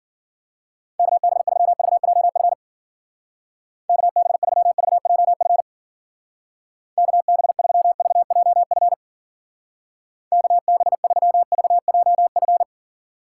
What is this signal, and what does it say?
Morse code, heard as a keyed tone, about 50 wpm, 700 Hz. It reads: KB3VJF KB3VJF KB3VJF KB3VJF